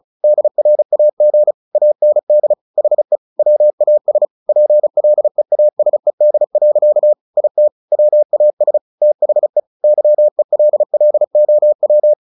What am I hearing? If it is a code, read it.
Morse code, heard as a keyed tone, about 34 wpm, 600 Hz. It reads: DRAG AND HE WAS PLEASED. IT WAS THE YELLOW
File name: the old man and the sea / the old man and the sea 1 35wpm